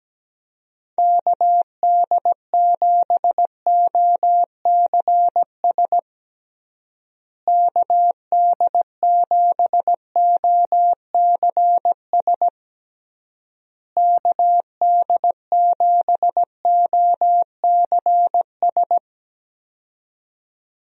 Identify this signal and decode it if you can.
Morse code, heard as a keyed tone, about 17 wpm, 700 Hz. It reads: KD7OCS KD7OCS KD7OCS
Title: USA Call Signs - 6 Characters - Repeated 3x - 17wpm